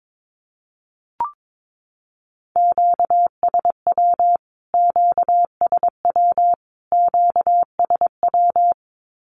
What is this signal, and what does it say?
Morse code, heard as a keyed tone, about 22 wpm, 700 Hz. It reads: QSW QSW QSW
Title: Q-Codes Repeated 3x 22wpm